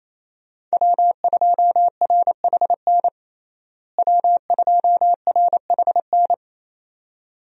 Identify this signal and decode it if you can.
Morse code, heard as a keyed tone, about 27 wpm, 700 Hz. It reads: W2RHN W2RHN
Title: USA Call Signs - 5 Characters - Repeated 2x - 28wpm